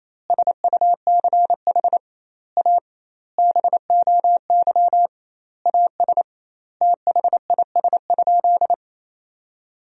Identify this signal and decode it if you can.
Morse code, heard as a keyed tone, about 28 wpm, 700 Hz. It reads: SUCH A BOY AS THIS?
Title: 2x Word-Spacing - Sentences from Top 400 Words - 28wpm